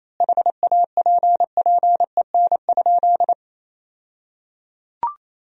Morse code, heard as a keyed tone, about 27 wpm, 700 Hz. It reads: HAPPEN?
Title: 2x Word-Spacing - Sentences from Top 400 Words - 28wpm